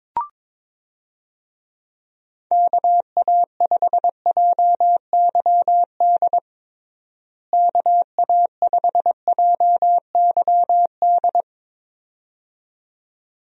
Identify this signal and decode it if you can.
Morse code, heard as a keyed tone, about 21 wpm, 700 Hz. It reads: KA5JYD KA5JYD
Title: USA Call Signs - 6 Characters - Repeated 2x - 22wpm